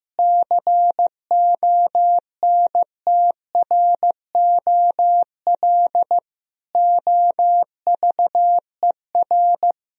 Morse code, heard as a keyed tone, about 15 wpm, 700 Hz. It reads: CONTROL OVER
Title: Sentences from Top 1000 Words - Review Part 3 - 15wpm